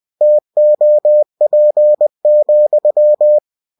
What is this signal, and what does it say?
Morse code, heard as a keyed tone, about 20 wpm, 600 Hz. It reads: TOP,
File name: cw-books / winnie-the-pooh / winnie-20wpm-page-0039